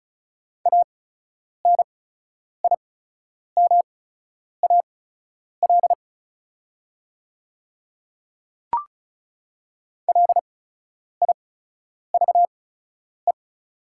Morse code, heard as a keyed tone, about 36 wpm, spaced as for 10 wpm, 690 Hz. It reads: ANIMAL LIVE
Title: 8x Character Spacing - Top 300 Words - 35wpm